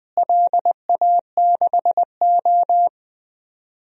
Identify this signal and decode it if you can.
Morse code, heard as a keyed tone, about 20 wpm, 700 Hz. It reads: LA6O
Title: DXCC-100 Call Signs - 4 Characters - Repeated 3x - 20wpm